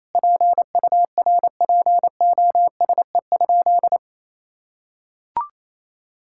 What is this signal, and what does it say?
Morse code, heard as a keyed tone, about 28 wpm, 700 Hz. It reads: PURPOSE?
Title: Sentences from Top 1000 Words - Review Part 5 - 28wpm